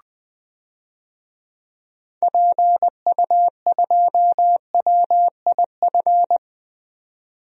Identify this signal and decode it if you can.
Morse code, heard as a keyed tone, about 20 wpm, 700 Hz. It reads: PU2WIF
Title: DXCC-100 Call Signs - 6 Characters - 20wpm